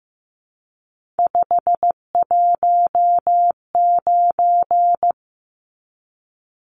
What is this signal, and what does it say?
Morse code, heard as a keyed tone, about 15 wpm, 700 Hz. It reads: 519
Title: Numbers - 3-Digit 15wpm